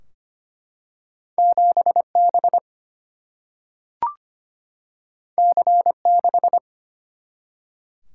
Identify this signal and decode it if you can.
Morse code, heard as a keyed tone, about 25 wpm, 700 Hz. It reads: 7 B C 6
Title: Single-Letter Single-Number 25wpm